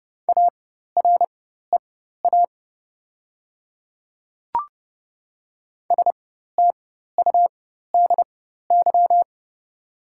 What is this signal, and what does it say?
Morse code, heard as a keyed tone, about 30 wpm, spaced as for 14 wpm, 700 Hz. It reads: AREA STUDY
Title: 4x Character Spacing - Top 500 Words - 30wpm